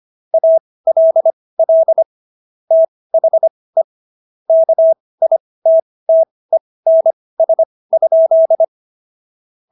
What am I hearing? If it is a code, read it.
Morse code, heard as a keyed tone, about 25 wpm, spaced as for 18 wpm, 650 Hz. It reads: ALL THE KITTENS?